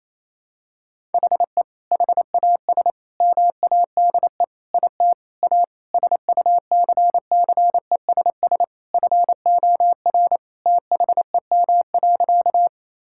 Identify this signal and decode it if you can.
Morse code, heard as a keyed tone, about 28 wpm, 700 Hz. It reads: HE HAS MADE IT A SUCCESS FOR THEM.
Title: Sentences from Top 1000 Words - Review Part 6 - 28wpm